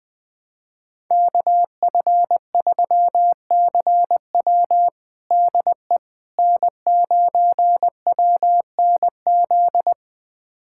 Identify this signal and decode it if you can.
Morse code, heard as a keyed tone, about 20 wpm, 700 Hz. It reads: KF3CW DE N9WNZ
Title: USA Callsign DE Callsign - 5 Characters - 20wpm